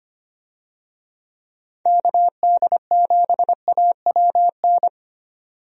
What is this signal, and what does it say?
Morse code, heard as a keyed tone, about 25 wpm, 700 Hz. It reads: KD7AWN
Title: USA Call Signs - 6 Characters - 25wpm